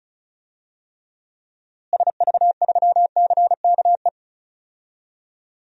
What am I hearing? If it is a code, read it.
Morse code, heard as a keyed tone, about 36 wpm, 700 Hz. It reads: SV3CKE
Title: DXCC-100 Call Signs - 6 Characters - Repeated 3x - 35wpm